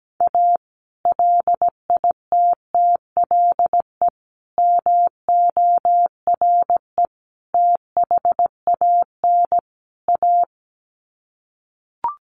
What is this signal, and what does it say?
Morse code, heard as a keyed tone, about 17 wpm, 700 Hz. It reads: A LITTLE MORE THAN A
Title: Sets of 5 Words Encore2 17wpm